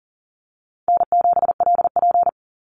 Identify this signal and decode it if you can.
Morse code, heard as a keyed tone, about 40 wpm, 700 Hz. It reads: N7LP
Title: NA SOTA Activator Callsigns 40wpm